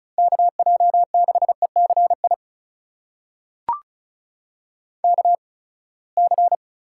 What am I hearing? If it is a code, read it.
Morse code, heard as a keyed tone, about 36 wpm, 700 Hz. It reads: K J 6 E C I K C
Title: USA Call Signs - 6 Characters - 8x Character Spacing - 35wpm